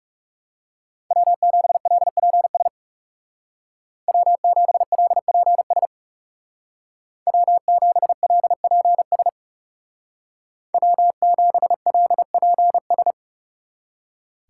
Morse code, heard as a keyed tone, about 37 wpm, 700 Hz. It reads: W7LPS W7LPS W7LPS W7LPS
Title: Speed Racing - USA 1x3 Call Signs 30wpm